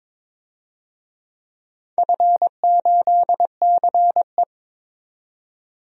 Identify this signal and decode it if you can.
Morse code, heard as a keyed tone, about 22 wpm, 700 Hz. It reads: F8CE
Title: DXCC-100 Call Signs - 4 Characters - 22wpm